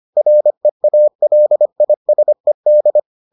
Morse code, heard as a keyed tone, about 25 wpm, 600 Hz. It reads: REALISED